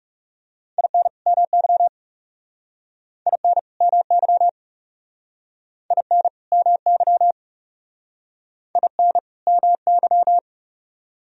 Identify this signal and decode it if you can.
Morse code, heard as a keyed tone, about 39 wpm, 700 Hz. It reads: IN MY IN MY IN MY IN MY